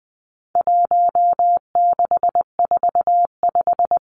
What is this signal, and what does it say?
Morse code, heard as a keyed tone, about 20 wpm, 700 Hz. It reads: 1645
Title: Numbers - 4-Digit 20wpm